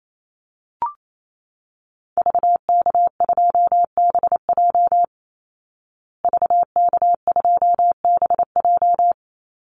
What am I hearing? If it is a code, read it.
Morse code, heard as a keyed tone, about 28 wpm, 700 Hz. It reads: VK2BJ VK2BJ
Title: Call Signs Repeated 2x - 28wpm